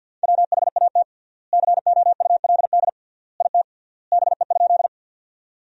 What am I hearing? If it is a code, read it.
Morse code, heard as a keyed tone, about 52 wpm, 700 Hz. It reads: WHAT COULD IT BE?
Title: 3x Word-Spacing - Sentences from Top 100 Words - 50wpm